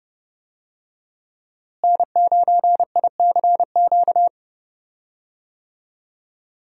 Morse code, heard as a keyed tone, about 30 wpm, 700 Hz. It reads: N9ICQ
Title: USA Call Signs - 5 Characters - Repeated 2x - 30wpm